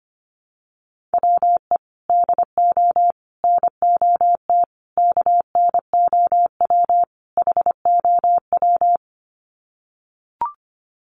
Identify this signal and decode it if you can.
Morse code, heard as a keyed tone, about 25 wpm, 700 Hz. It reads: WE DO NOT KNOW HOW
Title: Sets of 5 Words Encore 25wpm